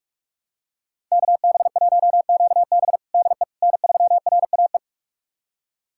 Morse code, heard as a keyed tone, about 41 wpm, 700 Hz. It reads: KB1QB DE N3RAE
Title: USA Callsign DE Callsign - 5 Characters - 45wpm